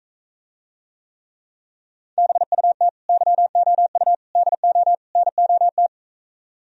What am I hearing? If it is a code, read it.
Morse code, heard as a keyed tone, about 45 wpm, 700 Hz. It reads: BUT YOU DO NOT